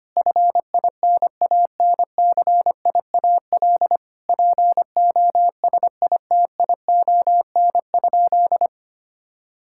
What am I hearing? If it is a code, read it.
Morse code, heard as a keyed tone, about 25 wpm, 700 Hz. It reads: FINANCIAL POSITION?
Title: Sentences from Top 1000 Words - Review Part 5 - 25wpm